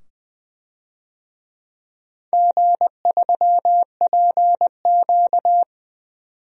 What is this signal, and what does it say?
Morse code, heard as a keyed tone, about 21 wpm, 700 Hz. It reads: G3PQ
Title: DXCC-100 Call Signs - 4 Characters - 20wpm